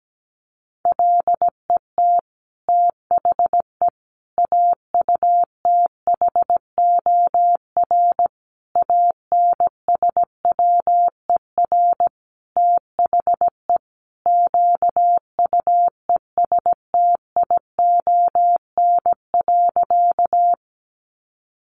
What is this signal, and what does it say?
Morse code, heard as a keyed tone, about 17 wpm, 700 Hz. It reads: LET THE AUTHOR ANSWER THE QUESTION.
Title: Sentences from Top 1200 Words 17wpm